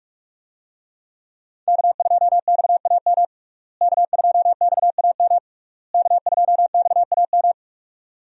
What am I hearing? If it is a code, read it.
Morse code, heard as a keyed tone, about 46 wpm, 700 Hz. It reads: K2XAM K2XAM K2XAM
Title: USA Call Signs - 5 Characters - Repeated 3x - 45wpm